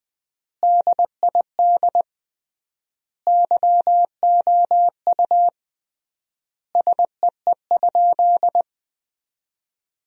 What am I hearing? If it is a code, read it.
Morse code, heard as a keyed tone, about 20 wpm, 700 Hz. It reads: DID YOU SEE?
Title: 3x Word-Spacing - Sentences from Top 100 Words - 20wpm